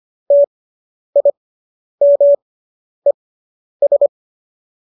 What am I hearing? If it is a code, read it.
Morse code, heard as a keyed tone, about 25 wpm, spaced as for 10 wpm, 570 Hz. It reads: TIMES